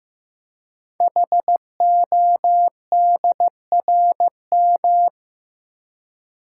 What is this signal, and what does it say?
Morse code, heard as a keyed tone, about 15 wpm, 700 Hz. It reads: HODRM
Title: Code Groups - 5 Letters - 15wpm